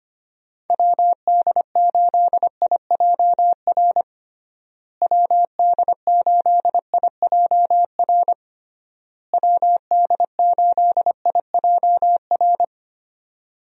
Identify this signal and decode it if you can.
Morse code, heard as a keyed tone, about 25 wpm, 700 Hz. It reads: WD8IJR WD8IJR WD8IJR
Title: USA Call Signs - 6 Characters - Repeated 3x - 25wpm